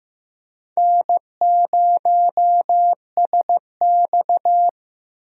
Morse code, heard as a keyed tone, about 15 wpm, 700 Hz. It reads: N0SX
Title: USA Call Signs - 4 Characters - Repeated 3x - 15wpm